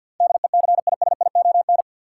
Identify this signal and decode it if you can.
Morse code, heard as a keyed tone, about 53 wpm, 700 Hz. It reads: DECISION
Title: Sentences from Top 1000 Words - New Word Drill - 50wpm